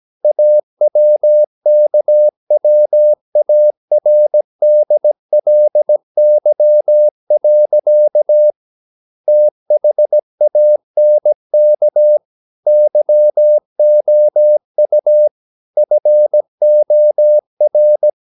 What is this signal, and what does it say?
Morse code, heard as a keyed tone, about 17 wpm, 600 Hz. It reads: AWKWARDLY. THANK YOU FOR
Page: Romanen Never af Ken Follet på engelsk.